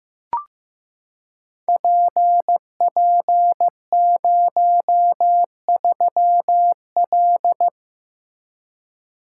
Morse code, heard as a keyed tone, about 15 wpm, 700 Hz. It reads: PP03L
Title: Code Groups - 5 Letter:Numbers - 15wpm